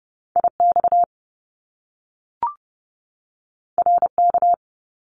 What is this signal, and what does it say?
Morse code, heard as a keyed tone, about 30 wpm, 700 Hz. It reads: IX RK
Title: Top Two Letter Combinations in Words 30wpm